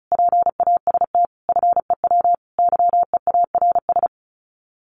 Morse code, heard as a keyed tone, about 34 wpm, 700 Hz. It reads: PAST FEW YEARS
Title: Sets of 5 Words 35wpm